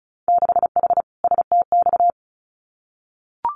Morse code, heard as a keyed tone, about 36 wpm, 700 Hz. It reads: 6H STX
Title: Winter Field Day Exchanges Encore 35wpm